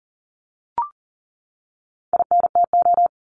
Morse code, heard as a keyed tone, about 40 wpm, 700 Hz. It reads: INTO
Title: Top 501-750 Words 40wpm